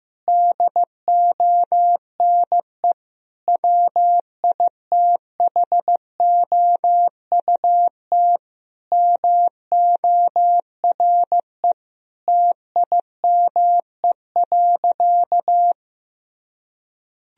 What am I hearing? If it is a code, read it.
Morse code, heard as a keyed tone, about 16 wpm, 700 Hz. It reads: DONE WITHOUT MORE TIME.
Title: Sentences from Top 1000 Words - Review Part 3 - 15wpm